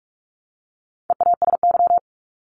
Morse code, heard as a keyed tone, about 44 wpm, 700 Hz. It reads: EASY
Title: Top 501-750 Words 45wpm